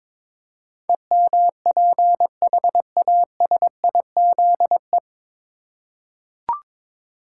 Morse code, heard as a keyed tone, about 22 wpm, 700 Hz. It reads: EMPHASIZE
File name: 9 Letter Words - First 3 Letters Given - 22wpm